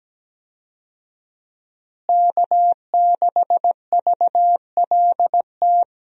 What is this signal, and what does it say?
Morse code, heard as a keyed tone, about 17 wpm, 700 Hz. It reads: K6VLT
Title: USA Call Signs - 5 Characters - Repeated 3x - 17wpm